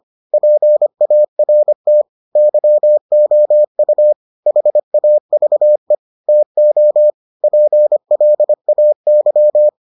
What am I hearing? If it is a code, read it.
Morse code, heard as a keyed tone, about 25 wpm, 600 Hz. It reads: PART YOU HAVE TO PLAY